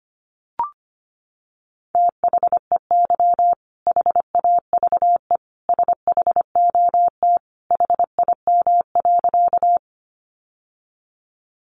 Morse code, heard as a keyed tone, about 25 wpm, 700 Hz. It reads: THEY HAVE SHOT HIM.
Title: Sentences from Top 1100 Words 25wpm